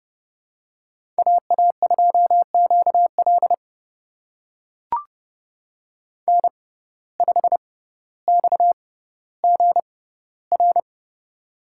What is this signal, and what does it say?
Morse code, heard as a keyed tone, about 30 wpm, spaced as for 15 wpm, 700 Hz. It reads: AA2QL N5XGR
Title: USA Call Signs - 5 Characters - 6x Character Spacing - 30wpm